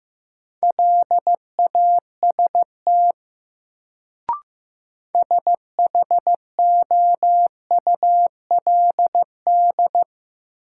Morse code, heard as a keyed tone, about 15 wpm, 700 Hz. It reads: LAST SHOULD
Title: Sentences from Top 300 Words - New Word Drill - 15wpm